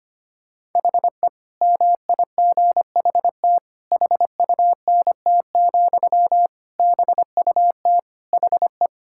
Morse code, heard as a keyed tone, about 25 wpm, 700 Hz. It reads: HE MIGHT HUNT, BUT HE
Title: Sentences from Top 1600 Words 25wpm